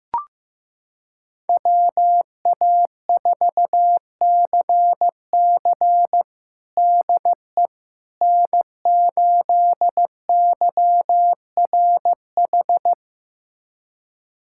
Morse code, heard as a keyed tone, about 15 wpm, 700 Hz. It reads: WA4CC DE N8YRH
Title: USA Callsign DE Callsign - 5 Characters - 15wpm